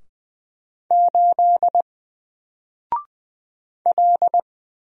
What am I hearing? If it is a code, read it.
Morse code, heard as a keyed tone, about 20 wpm, 700 Hz. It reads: 8 L
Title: Single Letter-Number 20wpm